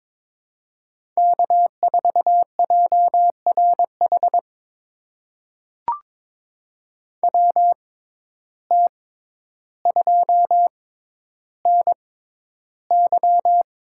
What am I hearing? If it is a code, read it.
Morse code, heard as a keyed tone, about 22 wpm, spaced as for 11 wpm, 700 Hz. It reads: K4JRH WT2NY
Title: USA Call Signs - 5 Characters - 6x Character Spacing - 22wpm